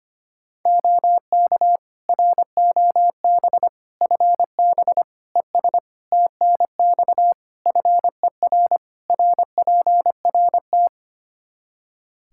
Morse code, heard as a keyed tone, about 25 wpm, 700 Hz. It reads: OK ROB FB ES TNX FER RPRT
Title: Practice QSO - Exchange 2 - Info - 25wpm